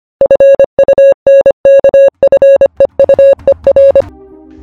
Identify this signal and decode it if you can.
Morse code, heard as a keyed tone, about 25 wpm, 550 Hz. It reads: FUNKFEUER
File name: funkfeuer / funkfeuer-files / WebSDR-Nuernberg-Teil-2